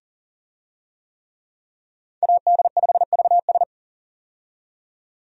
Morse code, heard as a keyed tone, about 37 wpm, 700 Hz. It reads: AD5VS
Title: USA Call Signs - 5 Characters - 40wpm